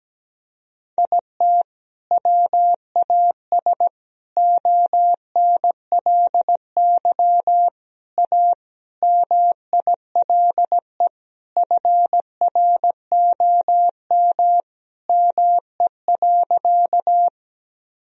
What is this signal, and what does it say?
Morse code, heard as a keyed tone, about 17 wpm, 700 Hz. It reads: IT WAS ONLY A MILE FROM ME.
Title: Sentences from Top 1000 Words - Review Part 1 - 17wpm